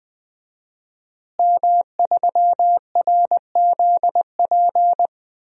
Morse code, heard as a keyed tone, about 21 wpm, 700 Hz. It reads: M3RZP
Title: DXCC-100 Call Signs - 5 Characters - Repeated 2x - 20wpm